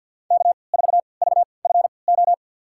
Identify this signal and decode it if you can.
Morse code, heard as a keyed tone, about 52 wpm, spaced as for 29 wpm, 700 Hz. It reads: K4VFO
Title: USA Call Signs - 3x Character Spacing - 50wpm